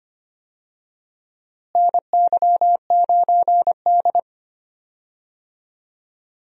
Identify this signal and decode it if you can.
Morse code, heard as a keyed tone, about 25 wpm, 700 Hz. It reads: NY9D
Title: USA Call Signs - 4 Characters - Repeated 2x - 25wpm